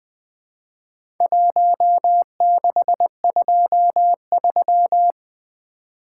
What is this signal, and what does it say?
Morse code, heard as a keyed tone, about 20 wpm, 700 Hz. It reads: 1623
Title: Four Letter-Number Combinations - 20wpm